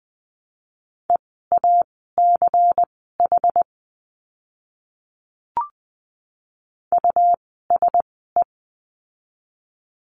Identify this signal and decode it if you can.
Morse code, heard as a keyed tone, about 20 wpm, spaced as for 14 wpm, 700 Hz. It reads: EACH USE
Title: 206 Lesson 33f - Letter j - Words 20wpm